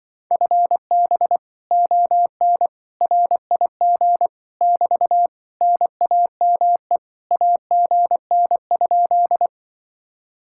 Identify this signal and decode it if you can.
Morse code, heard as a keyed tone, about 25 wpm, 700 Hz. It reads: FB ON RIG <BT> NAME AGN?